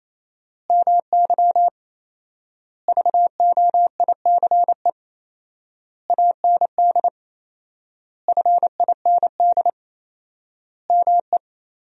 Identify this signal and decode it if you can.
Morse code, heard as a keyed tone, about 28 wpm, 700 Hz. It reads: MY VOICE AND FIND ME
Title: 4x Word-Spacing - Sentences from Top 500 Words - 28wpm